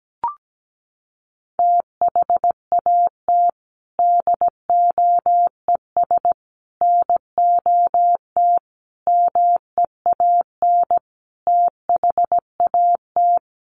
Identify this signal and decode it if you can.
Morse code, heard as a keyed tone, about 17 wpm, 700 Hz. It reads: THAT DOES NOT MEAN THAT
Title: Sets of 5 Words Encore 17wpm